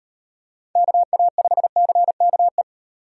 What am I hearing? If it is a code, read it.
Morse code, heard as a keyed tone, about 35 wpm, 700 Hz. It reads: KA5CKE